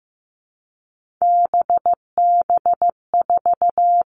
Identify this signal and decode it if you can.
Morse code, heard as a keyed tone, about 15 wpm, 700 Hz. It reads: BB4
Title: All Cut Numbers - 3-Digit 15wpm